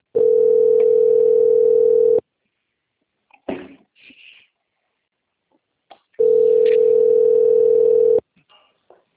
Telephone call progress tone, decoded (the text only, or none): ringback tone